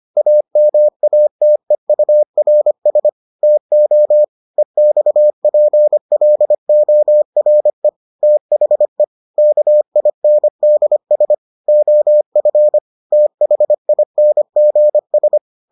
Morse code, heard as a keyed tone, about 25 wpm, 600 Hz. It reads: AMATEURS TO EXPLORE THE KINDS OF THINGS